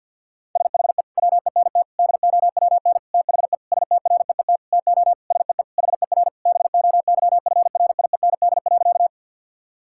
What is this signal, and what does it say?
Morse code, heard as a keyed tone, about 50 wpm, 700 Hz. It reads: SHE WENT DOWN THE STREET TO SEE HER BOYFRIEND.
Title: Sentences from Top 1000 Words - Review Part 5 - 50wpm